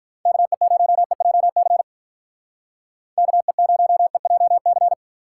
Morse code, heard as a keyed tone, about 46 wpm, 700 Hz. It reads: KE0EJC KE0EJC